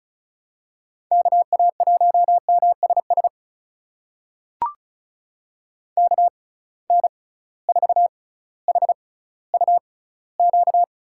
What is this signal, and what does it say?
Morse code, heard as a keyed tone, about 34 wpm, spaced as for 18 wpm, 700 Hz. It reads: KA1MSS KN4HUQ
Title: USA Call Signs - 6 Characters - 6x Character Spacing - 35wpm